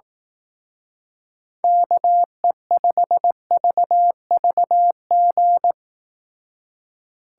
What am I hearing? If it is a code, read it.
Morse code, heard as a keyed tone, about 18 wpm, 700 Hz. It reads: KE5VVG
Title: Speed Racing - USA 2x3 Call Signs 15wpm